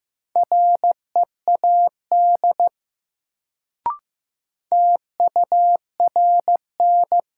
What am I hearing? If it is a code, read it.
Morse code, heard as a keyed tone, about 15 wpm, 700 Hz. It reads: READ TURN
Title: Sentences from Top 200 Words - New Word Drill - 15wpm